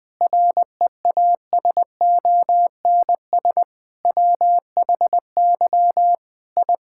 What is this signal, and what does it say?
Morse code, heard as a keyed tone, about 20 wpm, 700 Hz. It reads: REASONS WHY I
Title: Sentences with 5-Word Phrases - 20wpm